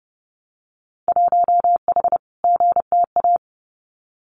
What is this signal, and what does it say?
Morse code, heard as a keyed tone, about 30 wpm, 700 Hz. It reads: 1H GTA
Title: Winter Field Day Exchanges 30wpm